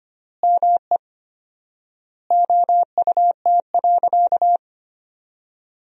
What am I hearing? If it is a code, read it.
Morse code, heard as a keyed tone, about 25 wpm, 700 Hz. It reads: ME OUT.
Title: 4x Word-Spacing - Sentences from Top 500 Words - 25wpm